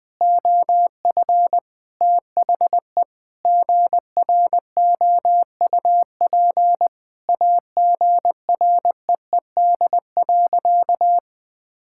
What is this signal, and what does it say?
Morse code, heard as a keyed tone, about 21 wpm, 700 Hz. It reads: OF THE GROUP AGREED.
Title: Sentences with 3-Word Phrases - 20wpm